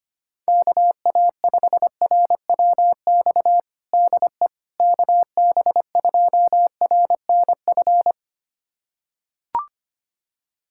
Morse code, heard as a keyed tone, about 25 wpm, 700 Hz. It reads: KA5RWX DE KB2RNF
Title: USA Callsign DE Callsign - 6 Characters - 25wpm